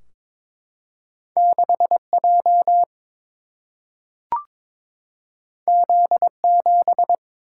Morse code, heard as a keyed tone, about 22 wpm, 700 Hz. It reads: 6 J Z 7
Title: Single-Letter Single-Number 22wpm